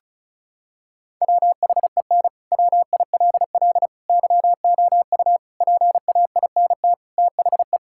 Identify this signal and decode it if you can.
Morse code, heard as a keyed tone, about 36 wpm, 700 Hz. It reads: WHEN WILL YOU PAINT THE
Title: Sentences from Top 1000 Words - Review Part 6 - 35wpm